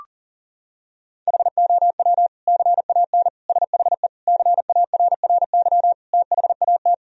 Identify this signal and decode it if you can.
Morse code, heard as a keyed tone, about 40 wpm, 700 Hz. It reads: HOW CAN SHE CARRY THAT
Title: Sentences from Top 1000 Words - Review Part 3 - 40wpm